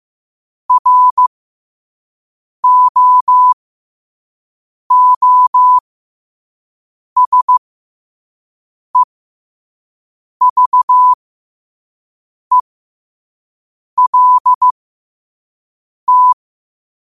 Morse code, heard as a keyed tone, about 15 wpm, spaced as for 5 wpm, 1000 Hz. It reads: ROOSEVELT
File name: qotdfiles / 5wpm